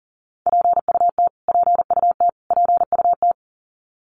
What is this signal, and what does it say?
Morse code, heard as a keyed tone, about 40 wpm, 700 Hz. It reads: PUT PUT PUT
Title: Top 500 Words Repeated 40wpm